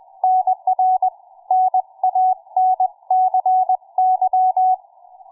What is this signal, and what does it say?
Morse code, heard as a keyed tone, about 20 wpm, 740 Hz. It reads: NR NANCY
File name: cw forum / fm5cd 1833 le17avril08 06h51